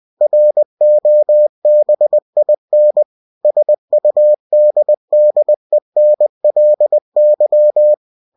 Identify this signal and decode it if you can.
Morse code, heard as a keyed tone, about 20 wpm, 600 Hz. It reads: ROBIN SUDDENLY
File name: cw-books / winnie-the-pooh / winnie-20wpm-page-0023